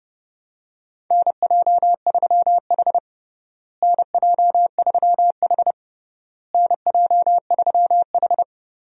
Morse code, heard as a keyed tone, about 32 wpm, 700 Hz. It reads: NJ3H NJ3H NJ3H
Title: USA Call Signs - 4 Characters - Repeated 3x - 30wpm